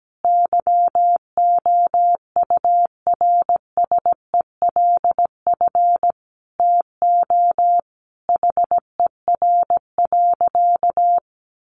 Morse code, heard as a keyed tone, about 17 wpm, 700 Hz. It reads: YOURSELF TO HER.
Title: Sentences from Top 1100 Words 17wpm